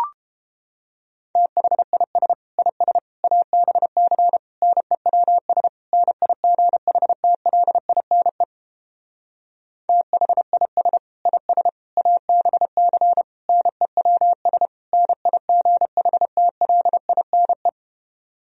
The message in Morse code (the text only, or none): THIS IS ABC NEWS NIGHTLINE THIS IS ABC NEWS NIGHTLINE